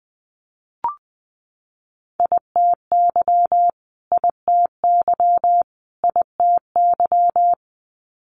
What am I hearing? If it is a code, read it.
Morse code, heard as a keyed tone, about 19 wpm, 700 Hz. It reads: ITY ITY ITY
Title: Multi-letter Phonemes Repeated 3x 20wpm